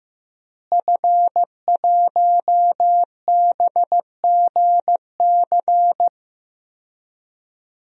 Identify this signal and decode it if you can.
Morse code, heard as a keyed tone, about 15 wpm, 700 Hz. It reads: F1BGC
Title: DXCC-100 Call Signs - 5 Characters - Repeated 3x - 15wpm